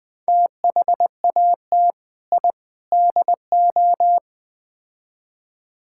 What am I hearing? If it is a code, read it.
Morse code, heard as a keyed tone, about 20 wpm, 700 Hz. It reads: THAT I DO
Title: Build A Phrase - 4-Word Phrases - 20wpm